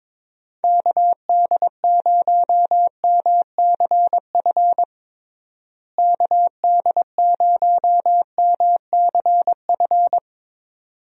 Morse code, heard as a keyed tone, about 22 wpm, 700 Hz. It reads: KD0MCF KD0MCF
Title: USA Call Signs - 6 Characters - Repeated 2x - 22wpm